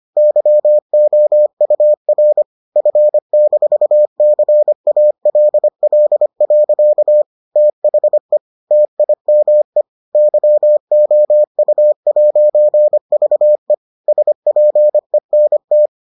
Morse code, heard as a keyed tone, about 25 wpm, 600 Hz. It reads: YOUR F-CALL. THE TIME YOU'VE SPENT